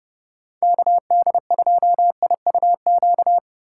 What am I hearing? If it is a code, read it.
Morse code, heard as a keyed tone, about 31 wpm, 700 Hz. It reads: KD2IUQ
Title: Speed Racing - USA 2x3 Call Signs 25wpm